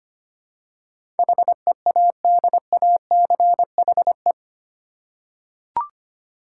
Morse code, heard as a keyed tone, about 25 wpm, 700 Hz. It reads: HEADACHE
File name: 8 Letter Words - First 3 Letters Given - 25wpm